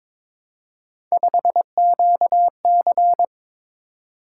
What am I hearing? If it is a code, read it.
Morse code, heard as a keyed tone, about 22 wpm, 700 Hz. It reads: 5QC
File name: practice / 22wpm / Three Letter-Number Combinations - 22wpm